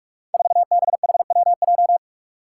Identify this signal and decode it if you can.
Morse code, heard as a keyed tone, about 44 wpm, 700 Hz. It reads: 4BHWJ
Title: Five Challenging Letter-Number Combinations 45wpm